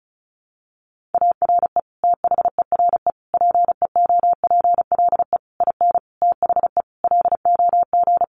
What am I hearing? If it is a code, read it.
Morse code, heard as a keyed tone, about 34 wpm, 700 Hz. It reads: ARE THERE PEOPLE IN THE LOG
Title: Sentences from Top 1400 Words 35wpm